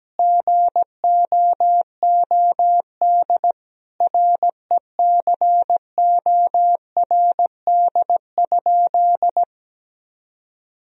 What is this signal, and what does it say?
Morse code, heard as a keyed tone, about 17 wpm, 700 Hz. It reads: GOOD RECORD?
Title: Sentences from Top 1000 Words - Review Part 3 - 17wpm